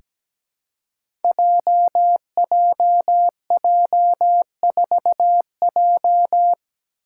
Morse code, heard as a keyed tone, about 17 wpm, 700 Hz. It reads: JJJ4J
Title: Five Challenging Letter-Number Combinations 17wpm